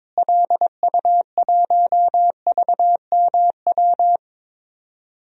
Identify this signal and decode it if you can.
Morse code, heard as a keyed tone, about 22 wpm, 700 Hz. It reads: LU1VMW
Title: DXCC-100 Call Signs - 6 Characters - Repeated 3x - 22wpm